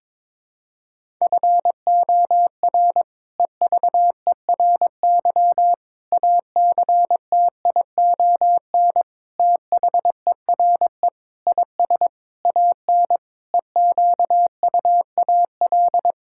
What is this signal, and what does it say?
Morse code, heard as a keyed tone, about 22 wpm, 700 Hz. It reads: FOR EVERY ACTION THERE IS AN EQUAL
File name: practice / 22wpm / Sets of 8 Words - 22wpm